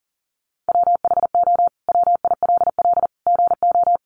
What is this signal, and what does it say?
Morse code, heard as a keyed tone, about 40 wpm, 700 Hz. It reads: WHO WILL GO
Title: Sentences from Top 100 Words 40wpm